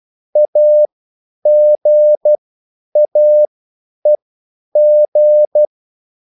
Morse code, heard as a keyed tone, about 12 wpm, spaced as for 9 wpm, 600 Hz. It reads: AGAEG
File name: QSL / CW lesson